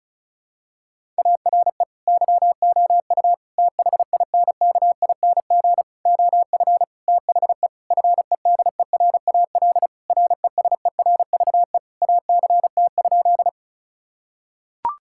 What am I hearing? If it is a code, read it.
Morse code, heard as a keyed tone, about 36 wpm, 700 Hz. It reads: ARE YOU THINKING OF THE FEDERAL RESERVE ACT?
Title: Sentences from Top 1000 Words - Review Part 6 - 35wpm